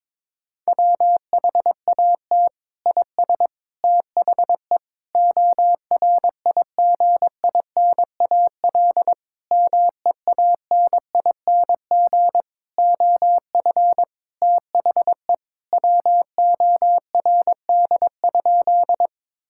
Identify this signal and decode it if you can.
Morse code, heard as a keyed tone, about 22 wpm, 700 Hz. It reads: WHAT IS THE ORIGINAL MEANING OF THE WORD?
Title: Sentences from Top 1000 Words - Review Part 6 - 22wpm